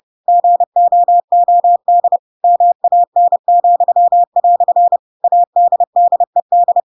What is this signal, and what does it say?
Morse code, heard as a keyed tone, about 30 wpm, 700 Hz. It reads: GOOD MAN," ADDED
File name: Pinocchio / Pinocchio ch3 30 30 ik1hge